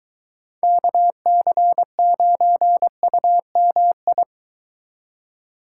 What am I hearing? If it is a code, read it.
Morse code, heard as a keyed tone, about 23 wpm, 700 Hz. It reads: KC9UMI